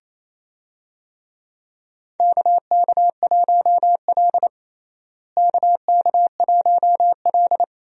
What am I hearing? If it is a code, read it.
Morse code, heard as a keyed tone, about 28 wpm, 700 Hz. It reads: KK1L KK1L
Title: USA Call Signs - 4 Characters - Repeated 2x - 28wpm